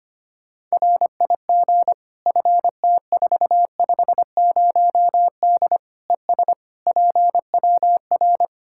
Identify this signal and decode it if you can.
Morse code, heard as a keyed tone, about 25 wpm, 700 Hz. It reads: RIG FT450D ES PWR
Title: K1USN SST QSO Practice 25wpm